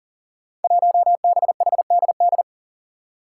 Morse code, heard as a keyed tone, about 40 wpm, 700 Hz. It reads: 1BHDD
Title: Five Challenging Letter-Number Combinations 40wpm